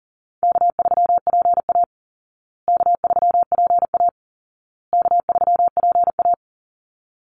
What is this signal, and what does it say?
Morse code, heard as a keyed tone, about 40 wpm, 700 Hz. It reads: K3PA K3PA K3PA
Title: Call Signs Repeated 2x - 40wpm